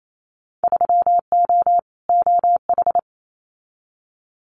Morse code, heard as a keyed tone, about 28 wpm, 700 Hz. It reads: 3O OH
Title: Winter Field Day Exchanges 28wpm